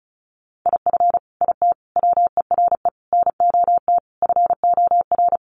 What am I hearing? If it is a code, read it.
Morse code, heard as a keyed tone, about 34 wpm, 700 Hz. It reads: IF IT WERE NOT FOR
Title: Sets of 5 Words 35wpm